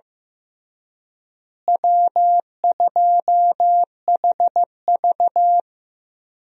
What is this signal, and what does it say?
Morse code, heard as a keyed tone, about 15 wpm, 700 Hz. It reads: W2HV
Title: USA Call Signs - 4 Characters - Repeated 3x - 15wpm